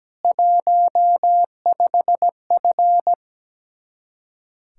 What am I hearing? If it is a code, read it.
Morse code, heard as a keyed tone, about 17 wpm, 700 Hz. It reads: 15F
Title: Speed-Racing Common QSO Phrases - 17wpm